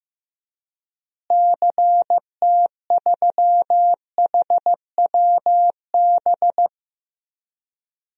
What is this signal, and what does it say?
Morse code, heard as a keyed tone, about 15 wpm, 700 Hz. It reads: CT3HWB
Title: DXCC-100 Call Signs - 6 Characters - Repeated 3x - 15wpm